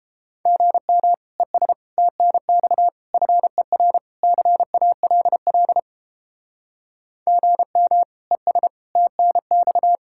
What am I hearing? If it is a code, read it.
Morse code, heard as a keyed tone, about 29 wpm, 700 Hz. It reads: GM ES TNX FER CALL GM ES TNX